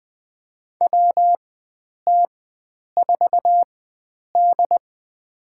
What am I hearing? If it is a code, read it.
Morse code, heard as a keyed tone, about 20 wpm, spaced as for 9 wpm, 700 Hz. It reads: WT4D
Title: USA Call Signs - 4 Characters - 4x Character Spacing - 20wpm